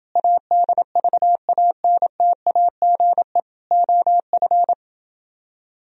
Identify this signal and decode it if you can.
Morse code, heard as a keyed tone, about 28 wpm, 700 Hz. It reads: ADVANTAGE OF